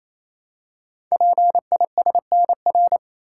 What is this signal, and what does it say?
Morse code, heard as a keyed tone, about 28 wpm, 700 Hz. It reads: PISNR
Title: Code Groups - 5 Letters - 28wpm